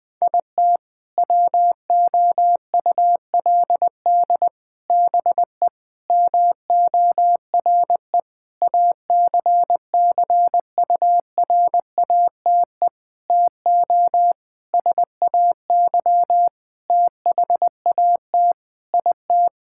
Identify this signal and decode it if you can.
Morse code, heard as a keyed tone, about 19 wpm, 700 Hz. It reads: IT WOULD BE MORE ACCURATE TO SAY THAT IT
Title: Sentences with 8-Word Phrases - 20wpm